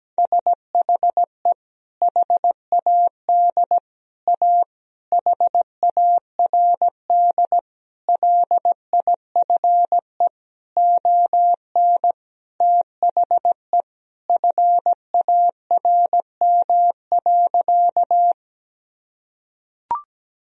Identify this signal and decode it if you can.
Morse code, heard as a keyed tone, about 17 wpm, 700 Hz. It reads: SHE HAD A HARD LIFE ON THE FARM.
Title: Sentences from Top 1000 Words - Review Part 2 - 17wpm